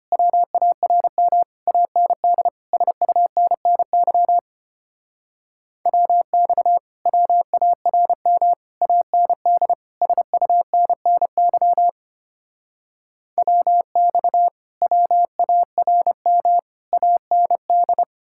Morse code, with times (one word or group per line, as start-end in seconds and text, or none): 0.1-1.4 WARM
1.7-2.5 AND
2.7-4.4 SUNNY
5.9-6.8 WX
7.1-8.5 WARM
8.8-9.7 AND
10.0-11.9 SUNNY
13.4-14.5 WX
14.8-16.6 WARM
16.9-18.0 AND